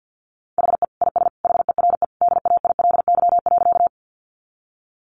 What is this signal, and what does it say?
Morse code, heard as a keyed tone, about 50 wpm, 700 Hz. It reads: HE IS HERE DAILY.
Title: Sentences from Top 1100 Words 50wpm